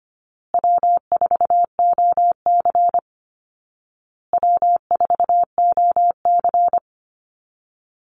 Morse code, heard as a keyed tone, about 25 wpm, 700 Hz. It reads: W4OC W4OC
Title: Easy Call Signs Repeated 3x - 25wpm